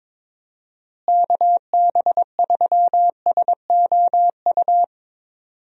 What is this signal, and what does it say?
Morse code, heard as a keyed tone, about 22 wpm, 700 Hz. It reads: KB3SOU
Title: USA Call Signs - 6 Characters - Repeated 3x - 22wpm